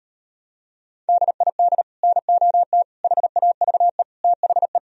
Morse code, heard as a keyed tone, about 38 wpm, 700 Hz. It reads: DID NOT HAVE THE